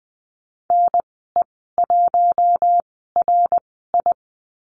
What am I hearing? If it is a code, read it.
Morse code, heard as a keyed tone, about 20 wpm, spaced as for 14 wpm, 700 Hz. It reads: NE1RI
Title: 069 Lesson 12g - Letter d - Call Signs 20wpm